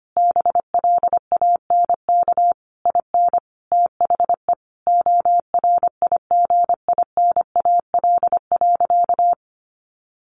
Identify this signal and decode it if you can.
Morse code, heard as a keyed tone, about 25 wpm, 700 Hz. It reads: BLANK IN THE ORIGINAL.
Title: Sentences from Top 1500 Words 25wpm